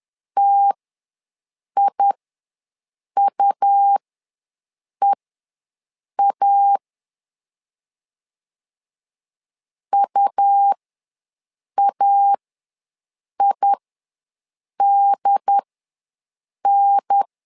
Morse code, heard as a keyed tone, about 10 wpm, spaced as for 6 wpm, 800 Hz. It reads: TIUEA UAIDN
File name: 2 UHDN ws 10wpm